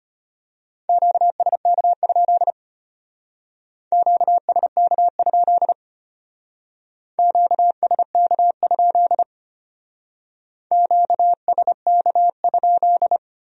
Morse code, heard as a keyed tone, about 30 wpm, 700 Hz. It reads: QSK? QSK? QSK? QSK?
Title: Speed-Racing Q-Codes - 25wpm